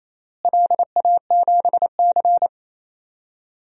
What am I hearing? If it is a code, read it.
Morse code, heard as a keyed tone, about 28 wpm, 700 Hz. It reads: LA7C
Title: DXCC-100 Call Signs - 4 Characters - Repeated 3x - 28wpm